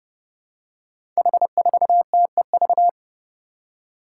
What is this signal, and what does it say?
Morse code, heard as a keyed tone, about 28 wpm, 700 Hz. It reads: H4TEV
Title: Five Challenging Letter-Number Combinations 30wpm